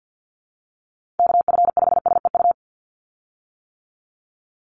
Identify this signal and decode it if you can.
Morse code, heard as a keyed tone, about 50 wpm, 700 Hz. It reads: KF5SEU
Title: USA Call Signs 50wpm